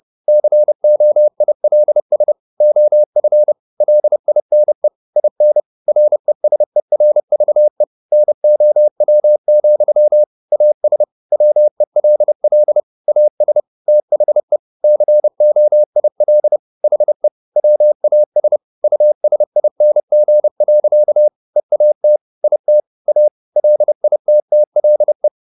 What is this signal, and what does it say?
Morse code, heard as a keyed tone, about 30 wpm, 600 Hz. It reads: COILS OF LINE IN RESERVE NOW, AS WELL AS THE COIL HE WAS USING. EAT IT A LITTLE
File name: the old man and the sea / the old man and the sea 1 30wpm